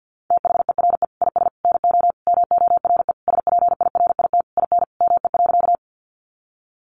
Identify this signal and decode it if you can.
Morse code, heard as a keyed tone, about 50 wpm, 700 Hz. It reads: THERE IS NO MORE SPIRIT IN ME.
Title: Sentences from Top 1400 Words 50wpm